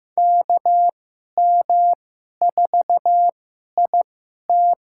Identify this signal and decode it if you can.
Morse code, heard as a keyed tone, about 15 wpm, spaced as for 11 wpm, 700 Hz. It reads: KM4IT
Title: USA Call Signs - 5 Characters - 2x Character Spacing - 15wpm